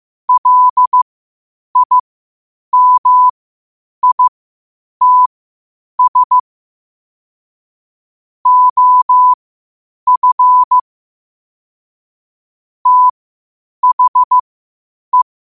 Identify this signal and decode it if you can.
Morse code, heard as a keyed tone, about 15 wpm, spaced as for 8 wpm, 1000 Hz. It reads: LIMITS OF THE
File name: qotdfiles / 8wpm